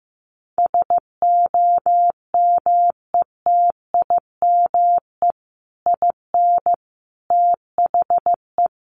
Morse code, heard as a keyed tone, about 15 wpm, 700 Hz. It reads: SOMETIME IN THE
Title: Sentences from Top 1500 Words 15wpm